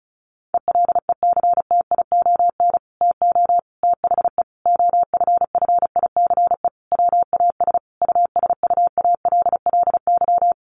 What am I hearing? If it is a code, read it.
Morse code, heard as a keyed tone, about 34 wpm, 700 Hz. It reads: ELECTION TO THE OFFICE WAS USUALLY
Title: Sentences from Top 1500 Words 35wpm